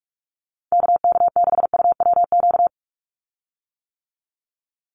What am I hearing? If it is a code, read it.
Morse code, heard as a keyed tone, about 46 wpm, 700 Hz. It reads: KK6UWQ
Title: USA Call Signs 45wpm